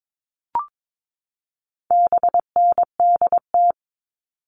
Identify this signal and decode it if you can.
Morse code, heard as a keyed tone, about 22 wpm, 700 Hz. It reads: BNDT
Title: All Cut Numbers - 4-Digit 22wpm